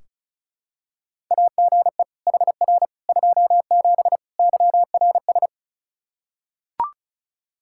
Morse code, heard as a keyed tone, about 36 wpm, 700 Hz. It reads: AGE HR 27 YRS
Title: Practice QSO - Exchange 3 - More Info - 35wpm